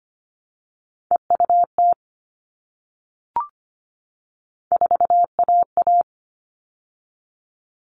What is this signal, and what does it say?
Morse code, heard as a keyed tone, about 25 wpm, 700 Hz. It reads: EUT 4AA
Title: All Cut Numbers - 3-Digit 25wpm